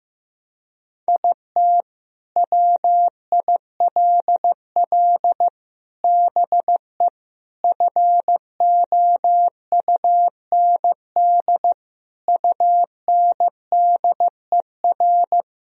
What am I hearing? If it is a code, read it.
Morse code, heard as a keyed tone, about 15 wpm, 700 Hz. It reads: IT WILL BE FOUND UNDER
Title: Sentences from Top 1000 Words - Review Part 1 - 15wpm